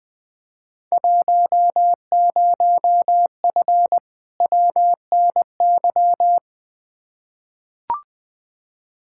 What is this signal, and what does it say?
Morse code, heard as a keyed tone, about 20 wpm, 700 Hz. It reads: 10F WNY
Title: Field Day Exchanges Encore - 20wpm